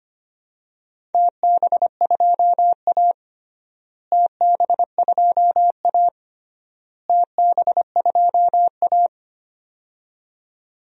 Morse code, heard as a keyed tone, about 25 wpm, 700 Hz. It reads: TB2A TB2A TB2A
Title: DXCC-100 Call Signs - 4 Characters - Repeated 3x - 25wpm